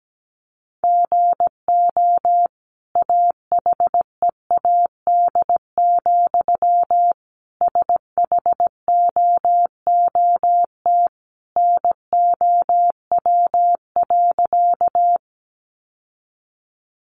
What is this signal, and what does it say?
Morse code, heard as a keyed tone, about 17 wpm, 700 Hz. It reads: GO AHEAD, SHOOT NOW.
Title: Sentences from Top 1200 Words 17wpm